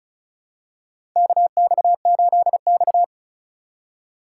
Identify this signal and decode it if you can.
Morse code, heard as a keyed tone, about 36 wpm, 700 Hz. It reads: KX8X
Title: K1USN SST QSO Practice - 35wpm